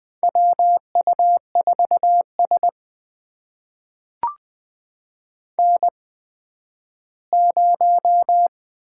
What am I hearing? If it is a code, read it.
Morse code, heard as a keyed tone, about 20 wpm, spaced as for 12 wpm, 700 Hz. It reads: WU4S N0
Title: USA Call Signs - 4 Characters - 8x Character Spacing - 20wpm